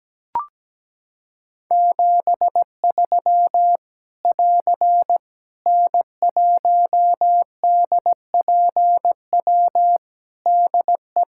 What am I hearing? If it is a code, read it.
Morse code, heard as a keyed tone, about 17 wpm, 700 Hz. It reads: 73 <AR> N1DPW DE
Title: K1USN SST QSO Practice 17wpm